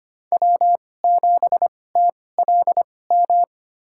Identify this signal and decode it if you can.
Morse code, heard as a keyed tone, about 25 wpm, spaced as for 18 wpm, 700 Hz. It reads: W7TLM
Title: USA Call Signs - 5 Characters - 2x Character Spacing - 25wpm